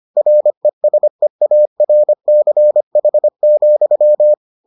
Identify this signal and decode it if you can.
Morse code, heard as a keyed tone, about 25 wpm, 600 Hz. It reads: RESEARCH,